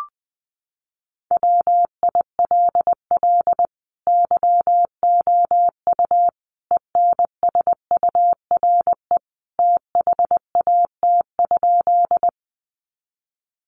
Morse code, heard as a keyed tone, about 20 wpm, 700 Hz. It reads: WILL YOU ENSURE THAT?
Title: Sentences from Top 1100 Words 20wpm